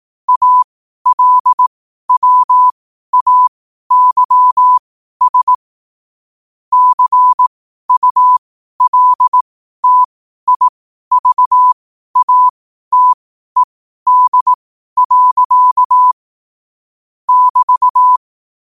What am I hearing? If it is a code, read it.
Morse code, heard as a keyed tone, about 18 wpm, spaced as for 12 wpm, 1000 Hz. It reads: ALWAYS CULTIVATED. <BT>
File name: qotdfiles / 12wpm